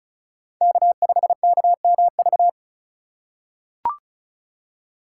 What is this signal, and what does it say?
Morse code, heard as a keyed tone, about 34 wpm, 700 Hz. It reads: K5KMV
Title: USA Call Signs - 5 Characters - 2x Character Spacing - 35wpm